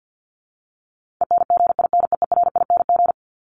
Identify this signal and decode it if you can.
Morse code, heard as a keyed tone, about 50 wpm, 700 Hz. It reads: ENGINEERING
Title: Top 1251-1500 Words 50wpm